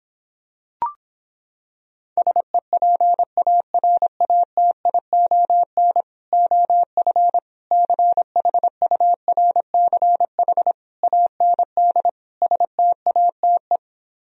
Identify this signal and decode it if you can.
Morse code, heard as a keyed tone, about 26 wpm, 700 Hz. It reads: SEPARATION OF CHURCH AND STATE